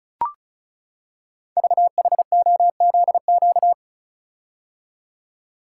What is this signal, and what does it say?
Morse code, heard as a keyed tone, about 36 wpm, 690 Hz. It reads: VHOZQ
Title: Code Groups - 5 Letters - 35wpm